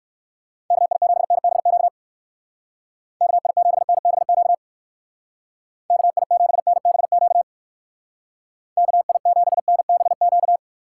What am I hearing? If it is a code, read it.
Morse code, heard as a keyed tone, about 56 wpm, 700 Hz. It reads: KI7NBQ KI7NBQ KI7NBQ KI7NBQ